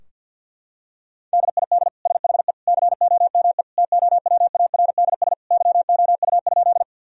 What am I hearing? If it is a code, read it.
Morse code, heard as a keyed tone, about 50 wpm, 700 Hz. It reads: DID SHE COME TOWARDS YOU?
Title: Sentences from Top 1000 Words - Review Part 2 - 50wpm